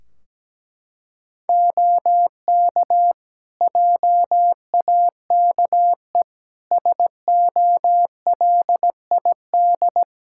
Morse code, heard as a keyed tone, about 17 wpm, 700 Hz. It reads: OK JAKE SOLID
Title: Practice QSO - Exchange 3 - More Info - 17wpm